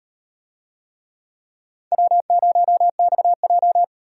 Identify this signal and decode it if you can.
Morse code, heard as a keyed tone, about 36 wpm, 700 Hz. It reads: W0XJ